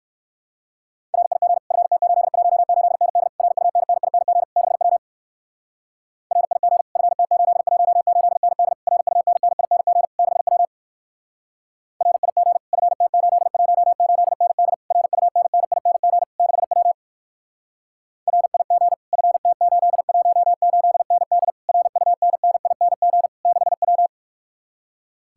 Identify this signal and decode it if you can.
Morse code, heard as a keyed tone, about 62 wpm, 690 Hz. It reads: RIG FT818ND RUNNING 6W RIG FT818ND RUNNING 6W RIG FT818ND RUNNING 6W RIG FT818ND RUNNING 6W